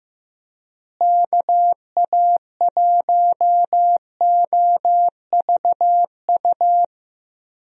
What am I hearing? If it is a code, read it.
Morse code, heard as a keyed tone, about 15 wpm, 700 Hz. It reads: KA1OVU
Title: USA Call Signs - 6 Characters - Repeated 3x - 15wpm